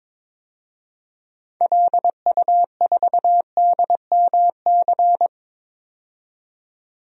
Morse code, heard as a keyed tone, about 21 wpm, 700 Hz. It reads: LU4DMC
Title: DXCC-100 Call Signs - 6 Characters - 22wpm